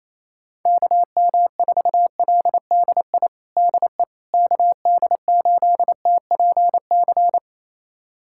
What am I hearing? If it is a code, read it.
Morse code, heard as a keyed tone, about 28 wpm, 700 Hz. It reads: KM4LDI DE KD8TPC
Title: USA Callsign DE Callsign - 6 Characters - 28wpm